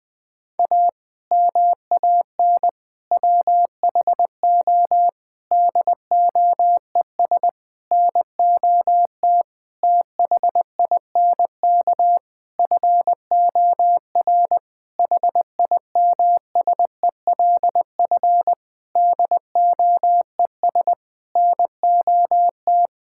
Morse code, heard as a keyed tone, about 19 wpm, 700 Hz. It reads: A MAN WHO DOES NOT THINK FOR HIMSELF DOES NOT
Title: Sentences from Top 1600 Words 20wpm